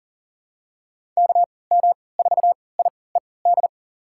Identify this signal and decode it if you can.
Morse code, heard as a keyed tone, about 40 wpm, spaced as for 23 wpm, 700 Hz. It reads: KM4IED
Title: USA Call Signs - 6 Characters - 3x Character Spacing - 40wpm